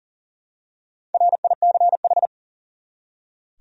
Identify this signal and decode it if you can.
Morse code, heard as a keyed tone, about 40 wpm, 700 Hz. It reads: RICH
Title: USA Common Names - Max 4 Letters - 40wpm